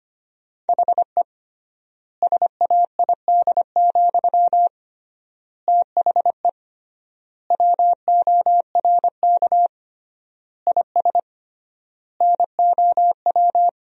Morse code, heard as a keyed tone, about 25 wpm, 700 Hz. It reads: HE SAID, THE WORK IS NOW
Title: 3x Word-Spacing - Sentences from Top 400 Words - 25wpm